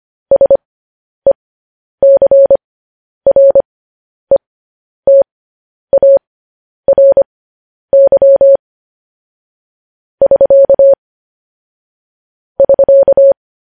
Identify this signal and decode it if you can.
Morse code, heard as a keyed tone, about 25 wpm, spaced as for 10 wpm, 550 Hz. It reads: SECRETARY <SK> <SK>
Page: news headlines in morse code at 10 wpm